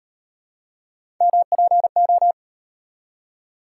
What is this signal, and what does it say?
Morse code, heard as a keyed tone, about 37 wpm, 700 Hz. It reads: MPO